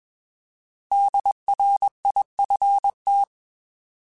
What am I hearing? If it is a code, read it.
Morse code, heard as a keyed tone, about 21 wpm, 800 Hz. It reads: DRIFT
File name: Alphabetical / Alphabetical mp3s / Ds @20